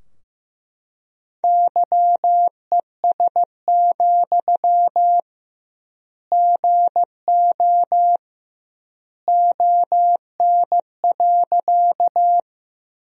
Morse code, heard as a keyed tone, about 15 wpm, 700 Hz. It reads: YES, GO ON.
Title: 2x Word-Spacing - Sentences from Top 500 Words - 15wpm